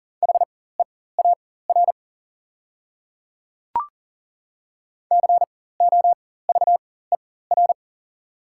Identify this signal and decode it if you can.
Morse code, heard as a keyed tone, about 37 wpm, spaced as for 18 wpm, 690 Hz. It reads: HEAR COVER
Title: 4x Character Spacing - Top 500 Words - 40wpm